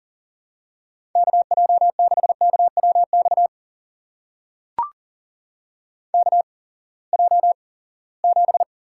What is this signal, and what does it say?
Morse code, heard as a keyed tone, about 41 wpm, spaced as for 23 wpm, 700 Hz. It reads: KJ6KWX KJ7
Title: USA Call Signs - 6 Characters - 8x Character Spacing - 40wpm